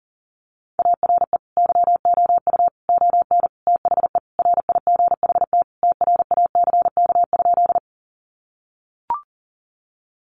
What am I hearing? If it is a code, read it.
Morse code, heard as a keyed tone, about 40 wpm, 700 Hz. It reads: ARE YOU ON THE RIGHT TRACK?
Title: Sentences from Top 1100 Words 40wpm